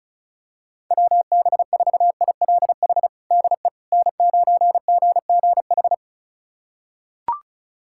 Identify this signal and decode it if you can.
Morse code, heard as a keyed tone, about 36 wpm, 700 Hz. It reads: WB4ILH DE N9GGH
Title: USA Callsign DE Callsign 35wpm